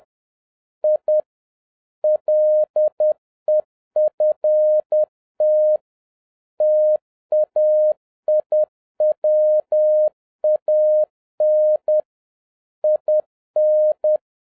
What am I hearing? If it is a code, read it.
Morse code, heard as a keyed tone, about 10 wpm, 610 Hz. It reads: I LEFT TAIWAN IN